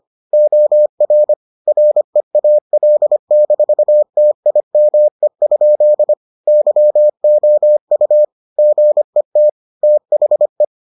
Morse code, heard as a keyed tone, about 25 wpm, 600 Hz. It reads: OR REAL-TIME? YOU GET THE